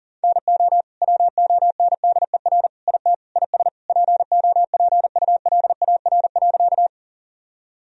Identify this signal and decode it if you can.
Morse code, heard as a keyed tone, about 40 wpm, 700 Hz. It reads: NO WONDER IT IS POPULAR.
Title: Sentences from Top 1000 Words - Review Part 3 - 40wpm